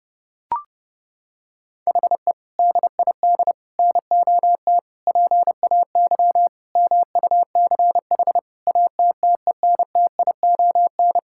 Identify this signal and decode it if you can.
Morse code, heard as a keyed tone, about 30 wpm, 700 Hz. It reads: HE DID NOT PAY MUCH ATTENTION
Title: Sentences from Top 1000 Words - Review Part 6 - 30wpm